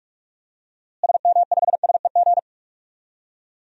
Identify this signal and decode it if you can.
Morse code, heard as a keyed tone, about 46 wpm, 700 Hz. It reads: SM5SEG
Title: DXCC-100 Call Signs - 6 Characters - Repeated 3x - 45wpm